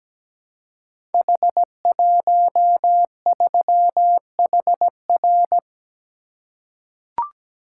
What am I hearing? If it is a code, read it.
Morse code, heard as a keyed tone, about 17 wpm, 700 Hz. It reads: H13HR
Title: Code Groups - 5 Letter:Numbers - 17wpm